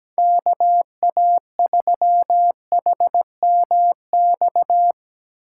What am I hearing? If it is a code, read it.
Morse code, heard as a keyed tone, about 17 wpm, 700 Hz. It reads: KA3HMX
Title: USA Call Signs - 6 Characters - 8x Character Spacing - 17wpm